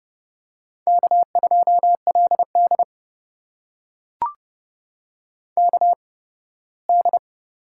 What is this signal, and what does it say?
Morse code, heard as a keyed tone, about 30 wpm, spaced as for 18 wpm, 700 Hz. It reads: K2LD KD
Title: USA Call Signs - 4 Characters - 8x Character Spacing - 30wpm